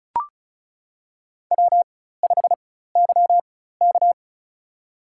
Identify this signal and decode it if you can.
Morse code, heard as a keyed tone, about 36 wpm, spaced as for 16 wpm, 700 Hz. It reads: W5YK
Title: USA Call Signs - 4 Characters - 4x Character Spacing - 35wpm